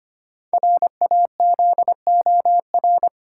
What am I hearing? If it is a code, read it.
Morse code, heard as a keyed tone, about 25 wpm, 690 Hz. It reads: RAZOR
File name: practice / 25wpm / Easy Words with Hints - 25wpm